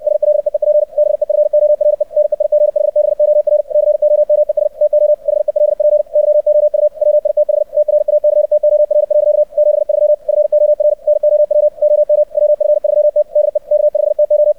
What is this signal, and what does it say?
Morse code, heard as a keyed tone, about 64 wpm, 600 Hz. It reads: SLEEP BEFORE NEIGHBOR COMES TO HELP YOU WITH TRACTOR. 73 FOR NOW OM WF3T DE K5TO
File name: archived / CRSnet